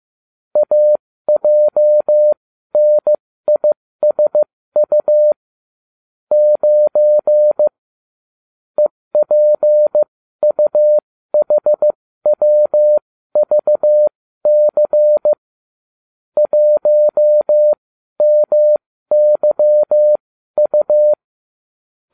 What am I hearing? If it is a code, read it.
Morse code, heard as a keyed tone, about 15 wpm, spaced as for 13 wpm, 600 Hz. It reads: AJNISU 9 EPUHWVC 1MYU